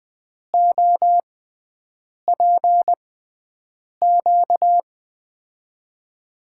Morse code, heard as a keyed tone, about 20 wpm, spaced as for 7 wpm, 700 Hz. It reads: OPQ
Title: USA Call Signs - 6 Characters - 6x Character Spacing - 20wpm